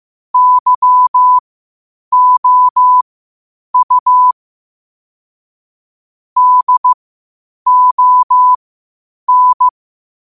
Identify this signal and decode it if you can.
Morse code, heard as a keyed tone, about 15 wpm, spaced as for 8 wpm, 1000 Hz. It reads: YOU DON
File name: qotdfiles / 8wpm